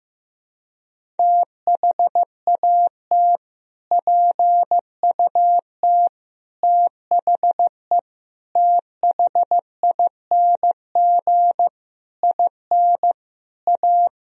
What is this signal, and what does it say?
Morse code, heard as a keyed tone, about 15 wpm, 700 Hz. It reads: THAT PUT THE THING IN A
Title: Sentences from Top 1000 Words - Review Part 1 - 15wpm